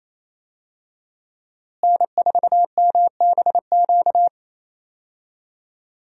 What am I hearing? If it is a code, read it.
Morse code, heard as a keyed tone, about 28 wpm, 700 Hz. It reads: N4MBQ
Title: USA Call Signs - 5 Characters - Repeated 3x - 28wpm